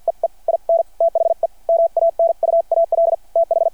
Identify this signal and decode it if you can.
Morse code, heard as a keyed tone, about 51 wpm, spaced as for 44 wpm, 660 Hz. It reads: EE IN THE MANUAL TH
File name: archived / CRSnet